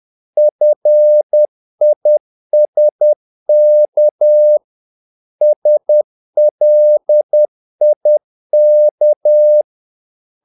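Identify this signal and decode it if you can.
Morse code, heard as a keyed tone, about 10 wpm, 600 Hz. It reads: FISK SLIK
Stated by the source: Morse practice of short Danish words at 10 wpm, from CW med Gnister.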